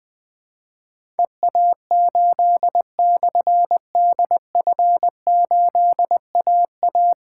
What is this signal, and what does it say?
Morse code, heard as a keyed tone, about 21 wpm, 700 Hz. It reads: EA8/DF8AA
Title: Difficult Call Signs 20wpm